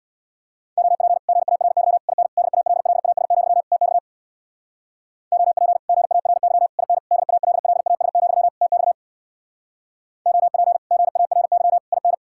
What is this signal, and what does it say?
Morse code, heard as a keyed tone, about 67 wpm, 700 Hz. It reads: OP GARY IN DALLAS, TX OP GARY IN DALLAS, TX OP GARY IN